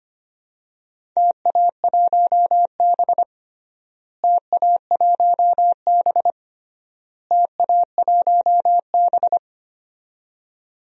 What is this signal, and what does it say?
Morse code, heard as a keyed tone, about 25 wpm, 700 Hz. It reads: TA1B TA1B TA1B
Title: DXCC-100 Call Signs - 4 Characters - Repeated 3x - 25wpm